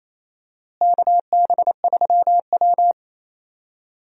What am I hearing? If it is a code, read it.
Morse code, heard as a keyed tone, about 28 wpm, 700 Hz. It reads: KB3W
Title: USA Call Signs - 4 Characters - Repeated 2x - 28wpm